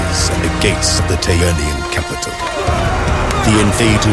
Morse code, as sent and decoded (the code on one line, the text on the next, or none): none